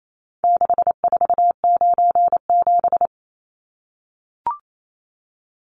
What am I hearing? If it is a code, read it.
Morse code, heard as a keyed tone, about 28 wpm, 700 Hz. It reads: 6497
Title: Numbers - 4-Digit 28wpm